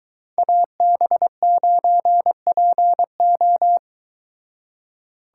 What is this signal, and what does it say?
Morse code, heard as a keyed tone, about 23 wpm, 700 Hz. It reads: AB9PO